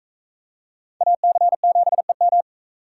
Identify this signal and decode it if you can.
Morse code, heard as a keyed tone, about 43 wpm, 700 Hz. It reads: AC7EM